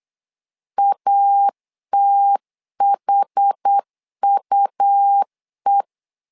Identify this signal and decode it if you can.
Morse code, heard as a keyed tone, about 8 wpm, 800 Hz. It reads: ATHUE
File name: LEZIONI / 5 LBVC / 5 LBV 8wpm 2837